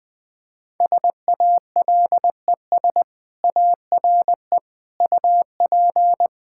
Morse code, heard as a keyed tone, about 20 wpm, 700 Hz. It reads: SALES ARE UP
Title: Sentences with 7-Word Phrases - 20wpm